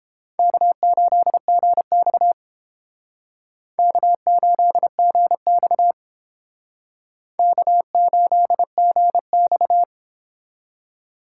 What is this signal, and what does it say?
Morse code, heard as a keyed tone, about 30 wpm, 700 Hz. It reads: K8GX K8GX K8GX